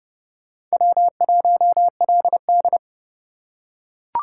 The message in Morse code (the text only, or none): W1LD